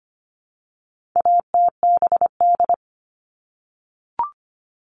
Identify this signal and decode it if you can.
Morse code, heard as a keyed tone, about 25 wpm, 700 Hz. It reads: ATBD
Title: All Cut Numbers - 4-Digit 25wpm